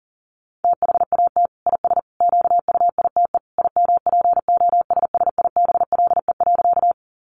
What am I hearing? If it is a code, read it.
Morse code, heard as a keyed tone, about 40 wpm, 700 Hz. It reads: THAT IS QUITE IMPOSSIBLE.
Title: Sentences from Top 1100 Words 40wpm